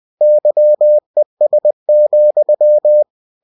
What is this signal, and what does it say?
Morse code, heard as a keyed tone, about 20 wpm, 600 Hz. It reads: YES,
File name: cw-books / winnie-the-pooh / winnie-20wpm-page-0084